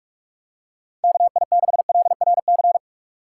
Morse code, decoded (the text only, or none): KI6LRC